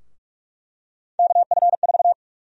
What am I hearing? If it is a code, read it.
Morse code, heard as a keyed tone, about 46 wpm, 700 Hz. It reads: KF4
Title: USA Call Sign Prefixes - 45wpm